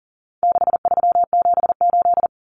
Morse code, heard as a keyed tone, about 37 wpm, 700 Hz. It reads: 6378
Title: Numbers - 4-Digit 40wpm